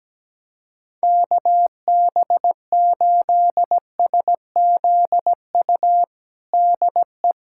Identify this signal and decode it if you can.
Morse code, heard as a keyed tone, about 17 wpm, 700 Hz. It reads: KB8SZU DE
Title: USA Callsign DE Callsign - 6 Characters - 17wpm